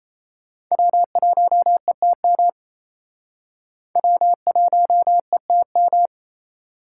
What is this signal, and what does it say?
Morse code, heard as a keyed tone, about 35 wpm, 700 Hz. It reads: W1ETM W1ETM
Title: Speed Racing - USA 1x3 Call Signs 28wpm